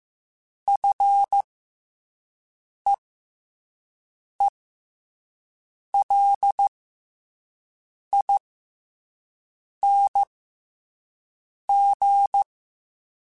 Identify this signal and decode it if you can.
Morse code, heard as a keyed tone, about 15 wpm, spaced as for 5 wpm, 800 Hz. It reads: FEELING
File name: F and L / F and L mp3 / F and L Prefixes 3x2 Twice @5